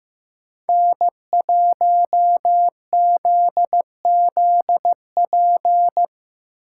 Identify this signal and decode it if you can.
Morse code, heard as a keyed tone, about 18 wpm, 700 Hz. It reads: N1ZZP
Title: Speed Racing - USA 1x3 Call Signs 15wpm